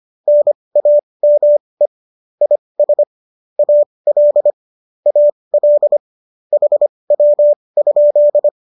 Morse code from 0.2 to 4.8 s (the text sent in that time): NAME IS AL